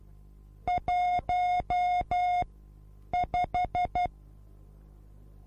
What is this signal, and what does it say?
Morse code, heard as a keyed tone, about 12 wpm, spaced as for 8 wpm, 720 Hz. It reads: 15
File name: repos / Morsekurs / DJ4UF Morsen24